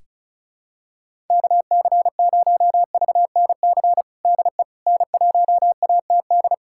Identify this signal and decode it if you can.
Morse code, heard as a keyed tone, about 36 wpm, 700 Hz. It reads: KC0VNC DE N1ATD
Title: USA Callsign DE Callsign 35wpm